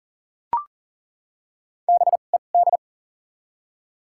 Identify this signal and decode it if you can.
Morse code, heard as a keyed tone, about 40 wpm, 700 Hz. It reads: B E D
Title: 2x Character Spacing - Top 500 Words - 40wpm